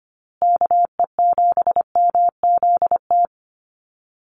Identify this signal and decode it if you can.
Morse code, heard as a keyed tone, about 25 wpm, 700 Hz. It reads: KE7MZT
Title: USA Call Signs Repeated 2x 25wpm